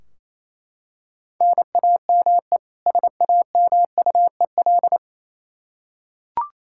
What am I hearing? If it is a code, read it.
Morse code, heard as a keyed tone, about 28 wpm, 700 Hz. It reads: NAME SAMUEL
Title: Practice QSO - Exchange 1 - Beginning - 28wpm